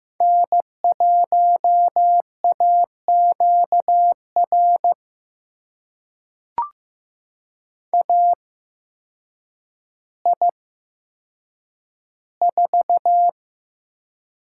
Morse code, heard as a keyed tone, about 15 wpm, spaced as for 8 wpm, 700 Hz. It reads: N1AQR AI4
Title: USA Call Signs - 8x Character Spacing - 15wpm